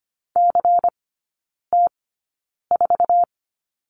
Morse code, heard as a keyed tone, about 25 wpm, spaced as for 9 wpm, 700 Hz. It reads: CT4
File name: course / 25:10wpm / 093 Lesson 16e - Letter c - Three Characters 25wpm